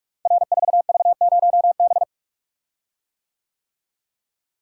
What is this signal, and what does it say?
Morse code, heard as a keyed tone, about 44 wpm, 700 Hz. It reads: R4V0B
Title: Code Groups - 5 Letter:Numbers - 45wpm